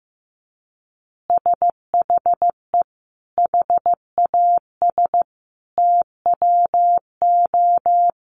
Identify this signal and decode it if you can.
Morse code, heard as a keyed tone, about 15 wpm, 700 Hz. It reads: SHE HAS TWO
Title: Additional Easy Phrases 15wpm